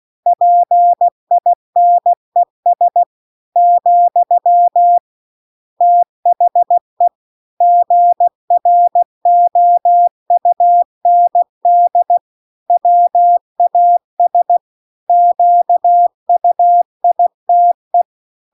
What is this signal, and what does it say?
Morse code, heard as a keyed tone, about 16 wpm, 700 Hz. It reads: PINES , THE GROUND WAS QUITE